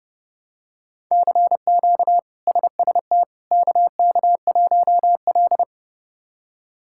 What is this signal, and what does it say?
Morse code, heard as a keyed tone, about 30 wpm, 700 Hz. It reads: CQ SST KK1L
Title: K1USN SST QSO Practice - 30wpm